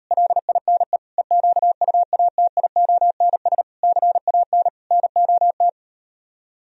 Morse code, heard as a keyed tone, about 38 wpm, 700 Hz. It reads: LINE EQUATIONS CAN NOT